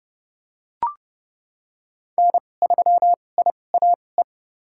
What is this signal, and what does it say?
Morse code, heard as a keyed tone, about 32 wpm, spaced as for 22 wpm, 700 Hz. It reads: N3IAE
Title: USA Call Signs - 2x Character Spacing - 30wpm